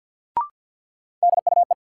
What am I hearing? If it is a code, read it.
Morse code, heard as a keyed tone, about 53 wpm, 700 Hz. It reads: DUE
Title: Sentences from Top 900 Words - New Word Drill - 50wpm